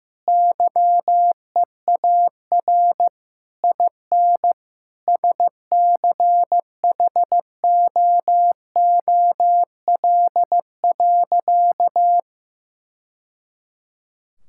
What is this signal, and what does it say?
Morse code, heard as a keyed tone, about 15 wpm, 700 Hz. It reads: YEAR IN SCHOOL.
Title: Sentences from Top 1000 Words - Review Part 5 - 15wpm